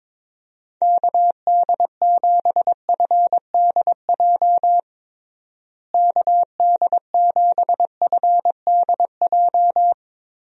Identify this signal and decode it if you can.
Morse code, heard as a keyed tone, about 22 wpm, 700 Hz. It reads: KD7FDJ KD7FDJ
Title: USA Call Signs - 6 Characters - Repeated 3x - 22wpm